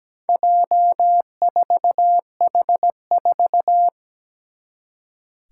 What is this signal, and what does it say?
Morse code, heard as a keyed tone, about 17 wpm, 700 Hz. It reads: J4H4
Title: Four Challenging Letter-Number Combinations 17wpm